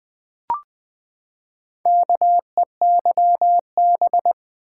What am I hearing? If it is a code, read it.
Morse code, heard as a keyed tone, about 20 wpm, 700 Hz. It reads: KEYB
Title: Build A Word - 4 Letters At A Time - 8-Letter Words - 20wpm